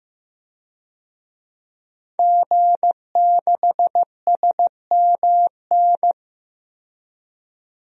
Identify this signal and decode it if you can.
Morse code, heard as a keyed tone, about 15 wpm, 700 Hz. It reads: G6SMN
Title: DXCC-100 Call Signs - 5 Characters - 15wpm